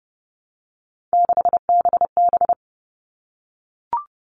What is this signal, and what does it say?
Morse code, heard as a keyed tone, about 30 wpm, 700 Hz. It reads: 6BB
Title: All Cut Numbers - 3-Digit 30wpm